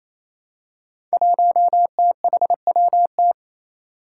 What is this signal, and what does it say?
Morse code, heard as a keyed tone, about 28 wpm, 700 Hz. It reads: 1THWT
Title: Five Challenging Letter-Number Combinations 28wpm